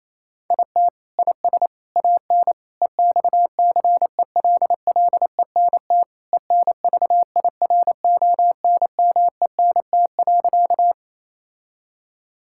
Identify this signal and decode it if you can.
Morse code, heard as a keyed tone, about 28 wpm, 700 Hz. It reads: IT IS AN EXCELLENT ENVIRONMENT.
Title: Sentences from Top 1000 Words - Review Part 5 - 28wpm